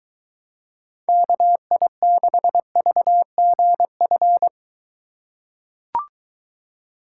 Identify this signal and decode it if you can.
Morse code, heard as a keyed tone, about 22 wpm, 700 Hz. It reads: KI6VGF